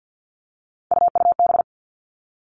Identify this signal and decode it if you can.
Morse code, heard as a keyed tone, about 50 wpm, 700 Hz. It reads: UUB
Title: Three Challenging Letter-Number Combinations 50wpm